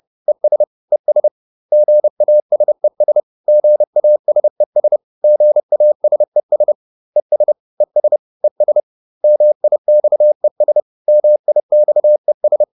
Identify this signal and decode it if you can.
Morse code, heard as a keyed tone, about 30 wpm, 600 Hz. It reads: ES ES GASES GASES GASES ES ES ES MIXES MIXES